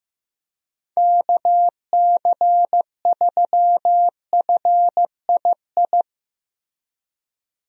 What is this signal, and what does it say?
Morse code, heard as a keyed tone, about 16 wpm, 700 Hz. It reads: KC3FII
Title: USA Call Signs - 6 Characters - Repeated 2x - 15wpm